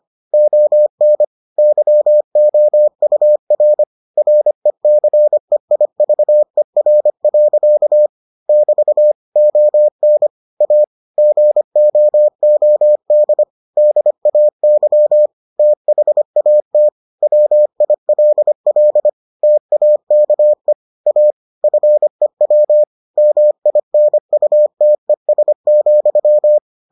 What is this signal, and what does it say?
Morse code, heard as a keyed tone, about 25 wpm, 600 Hz. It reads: ON YOUR RECEIVER. <BT> ON A GOOD DAY THAT WILL TAKE A FEW MINUTES,